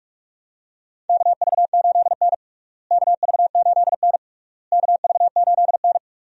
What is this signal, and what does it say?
Morse code, heard as a keyed tone, about 46 wpm, 700 Hz. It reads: KV8N KV8N KV8N
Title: USA Call Signs - 4 Characters - Repeated 3x - 45wpm